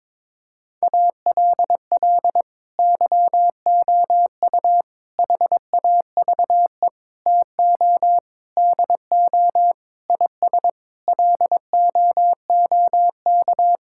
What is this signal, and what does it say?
Morse code, heard as a keyed tone, about 22 wpm, 700 Hz. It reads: ALL YOU HAVE TO DO IS LOOK
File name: practice / 22wpm / Sets of 8 Words - 22wpm